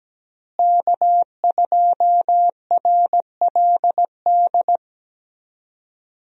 Morse code, heard as a keyed tone, about 17 wpm, 700 Hz. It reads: K2RLD
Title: USA Call Signs - 5 Characters - Repeated 2x - 17wpm